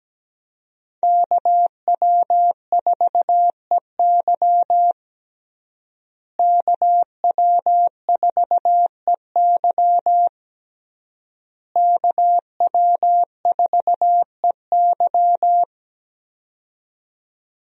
Morse code, heard as a keyed tone, about 17 wpm, 700 Hz. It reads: KW4EY KW4EY KW4EY
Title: USA Call Signs - 5 Characters - Repeated 3x - 17wpm